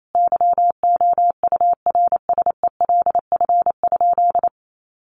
Morse code, heard as a keyed tone, about 28 wpm, 700 Hz. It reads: YOURSELF?
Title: Sentences from Top 1500 Words 28wpm